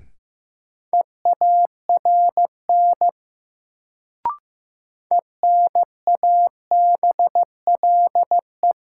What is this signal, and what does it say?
Morse code, heard as a keyed tone, about 15 wpm, 700 Hz. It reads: EARN ENABLE
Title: Sentences from Top 1500 Words - New Word Drill Alt - 15wpm